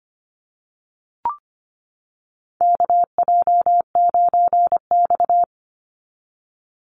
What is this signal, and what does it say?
Morse code, heard as a keyed tone, about 25 wpm, 700 Hz. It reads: KJ9X
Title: NA SOTA Chaser Callsigns 25wpm